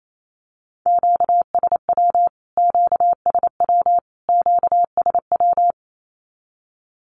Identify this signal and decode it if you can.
Morse code, heard as a keyed tone, about 28 wpm, 700 Hz. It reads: QSW QSW QSW
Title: Q-Codes Repeated 3x 28wpm